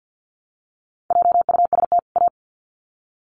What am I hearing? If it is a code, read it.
Morse code, heard as a keyed tone, about 50 wpm, 700 Hz. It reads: JUST A
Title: Sets of 2 Words 50wpm